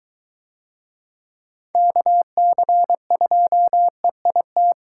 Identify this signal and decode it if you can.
Morse code, heard as a keyed tone, about 22 wpm, 700 Hz. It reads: KC2EIT